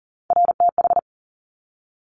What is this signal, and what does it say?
Morse code, heard as a keyed tone, about 38 wpm, 700 Hz. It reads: RTH
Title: Top Three Letter Combinations in Words 40wpm